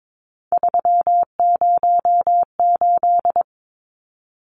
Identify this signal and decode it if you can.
Morse code, heard as a keyed tone, about 22 wpm, 700 Hz. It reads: 308
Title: Numbers - 3-Digit 22wpm